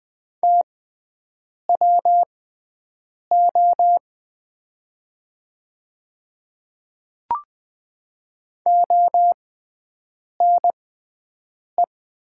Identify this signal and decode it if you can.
Morse code, heard as a keyed tone, about 20 wpm, spaced as for 7 wpm, 700 Hz. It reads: TWO ONE
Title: 6x Character Spacing - Top 100 Words - 20wpm